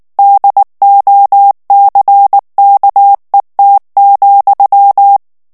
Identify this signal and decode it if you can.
Morse code, heard as a keyed tone, about 19 wpm, 800 Hz. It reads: DOCKET,
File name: morse mp3 / ANGLAIS texte clair 3 vitesse 18 mots minute